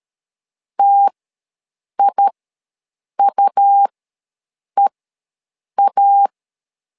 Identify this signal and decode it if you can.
Morse code, heard as a keyed tone, about 13 wpm, spaced as for 7 wpm, 800 Hz. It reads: TIUEA
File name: LEZIONI / 2 UHDN / 2 U ws 12wpm 1655